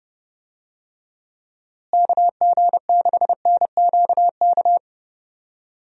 Morse code, heard as a keyed tone, about 30 wpm, 700 Hz. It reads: KG6NQK
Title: USA Call Signs - 6 Characters - Repeated 2x - 30wpm